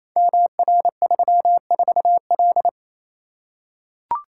Morse code, heard as a keyed tone, about 29 wpm, 700 Hz. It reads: MR34L
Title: Code Groups - 5 Letter:Numbers - 28wpm